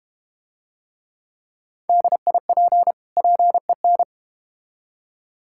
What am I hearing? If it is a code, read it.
Morse code, heard as a keyed tone, about 32 wpm, 700 Hz. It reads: DIP PEN